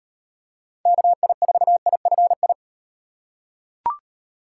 Morse code, heard as a keyed tone, about 38 wpm, 700 Hz. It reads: KI4IFI